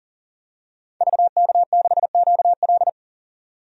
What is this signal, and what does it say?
Morse code, heard as a keyed tone, about 40 wpm, 700 Hz. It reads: VK6QL
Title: DXCC-100 Call Signs - 5 Characters - 40wpm